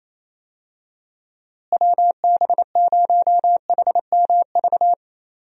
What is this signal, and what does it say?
Morse code, heard as a keyed tone, about 27 wpm, 700 Hz. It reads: WB0HMV
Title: USA Call Signs - 6 Characters - Repeated 3x - 28wpm